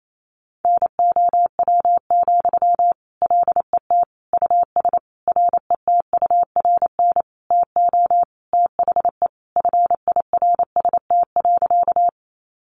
Morse code, heard as a keyed tone, about 28 wpm, 700 Hz. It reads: NOW, LET US RETURN TO THE FIRST.
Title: Sentences from Top 800 Words 28wpm